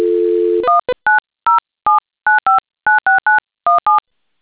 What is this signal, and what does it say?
DTMF, heard as telephone touch tones, tones of about 150 ms. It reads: 19079596917